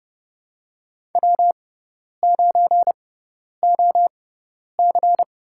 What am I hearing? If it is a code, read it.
Morse code, heard as a keyed tone, about 31 wpm, spaced as for 10 wpm, 700 Hz. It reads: W9OC
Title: USA Call Signs - 4 Characters - 6x Character Spacing - 30wpm